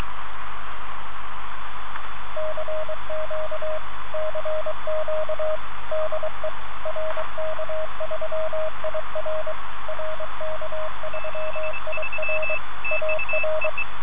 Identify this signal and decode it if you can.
Morse code, heard as a keyed tone, about 23 wpm, 620 Hz, with radio noise. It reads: CQ CQ DE RK3IR RK3IR AR